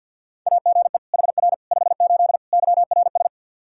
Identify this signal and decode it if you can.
Morse code, heard as a keyed tone, about 53 wpm, 700 Hz. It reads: AGE HR 58 YRS
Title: Practice QSO - Exchange 3 - More Info - 50wpm